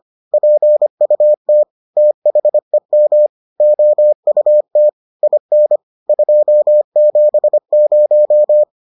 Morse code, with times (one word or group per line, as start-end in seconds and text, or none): 0.3-1.6 PUT
2.0-3.3 THEM
3.6-4.9 OUT
5.2-5.8 IN
6.1-8.6 270